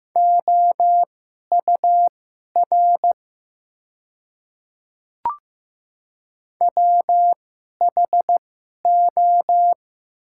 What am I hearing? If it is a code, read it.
Morse code, heard as a keyed tone, about 15 wpm, spaced as for 11 wpm, 700 Hz. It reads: OUR WHO
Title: 2x Character Spacing - Top 200 Words - 15wpm